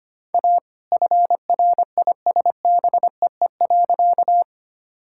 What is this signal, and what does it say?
Morse code, heard as a keyed tone, about 25 wpm, 700 Hz. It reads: A FRISBEE.
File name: practice / 25wpm / All-Age Appropriate Jokes - 25wpm